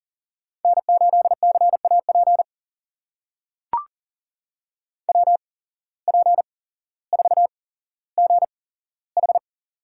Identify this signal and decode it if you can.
Morse code, heard as a keyed tone, about 40 wpm, spaced as for 17 wpm, 700 Hz. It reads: N8CAP WP4GH
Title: USA Call Signs - 5 Characters - 8x Character Spacing - 40wpm